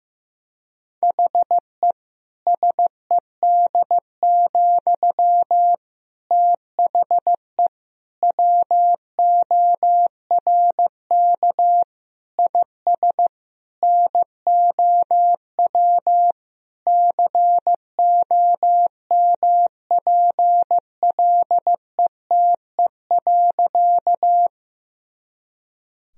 Morse code, heard as a keyed tone, about 15 wpm, 700 Hz. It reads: HE SED, THE WORK IS NOW COMPLETE.
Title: Sentences from Top 400 Words and Abbreviations 15wpm